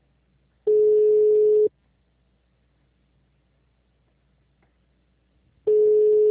Ringback tone (Europe, 425 Hz).